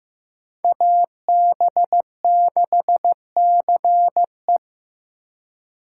Morse code, heard as a keyed tone, about 18 wpm, 700 Hz. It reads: AB6CE
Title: Speed Racing - USA 2x2 Call Signs 15wpm